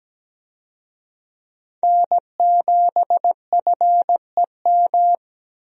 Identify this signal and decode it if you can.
Morse code, heard as a keyed tone, about 17 wpm, 700 Hz. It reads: N7FEM
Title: USA Call Signs - 5 Characters - 17wpm